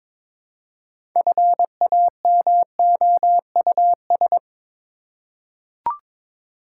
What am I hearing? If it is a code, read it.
Morse code, heard as a keyed tone, about 22 wpm, 700 Hz. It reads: FAMOUS
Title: Six Letter Words - 22wpm